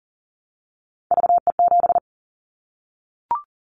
Morse code, heard as a keyed tone, about 40 wpm, 700 Hz. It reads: VE7
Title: Callsign Prefixes for Top 100 DXCC - 40wpm